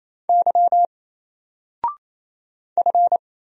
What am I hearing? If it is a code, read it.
Morse code, heard as a keyed tone, about 28 wpm, 700 Hz. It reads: Y F
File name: Single-Letter-Number-28wpm